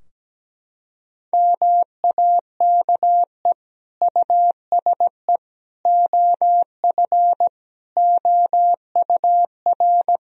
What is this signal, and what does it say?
Morse code, heard as a keyed tone, about 17 wpm, 700 Hz. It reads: MAKE USE OF OUR
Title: Sentences from Top 1000 Words - Review Part 4 - 17wpm